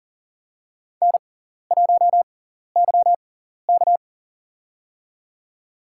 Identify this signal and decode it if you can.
Morse code, heard as a keyed tone, about 40 wpm, spaced as for 14 wpm, 700 Hz. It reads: N1YK
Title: USA Call Signs - 4 Characters - 6x Character Spacing - 40wpm